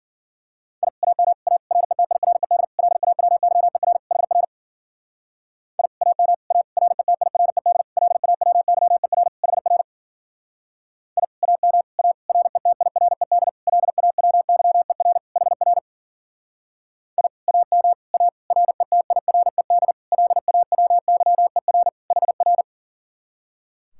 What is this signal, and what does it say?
Morse code, heard as a keyed tone, about 50 wpm, 700 Hz. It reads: I AM A RETIRED LAWYER HR I AM A RETIRED LAWYER HR I AM A RETIRED LAWYER HR I AM A RETIRED LAWYER HR
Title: Speed-Racing Common QSO Phrases Encore - 40wpm